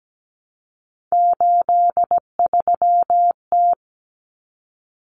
Morse code, heard as a keyed tone, about 17 wpm, 690 Hz. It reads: 83T
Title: Common Cut Numbers - 3-Digit 17wpm